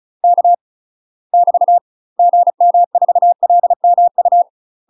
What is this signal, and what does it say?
Morse code, heard as a keyed tone, about 34 wpm, 700 Hz. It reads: K <BT> GM4LMU
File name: UK MASTER QSOs TRAINING / 35WPM / UK MASTER QSOs 0034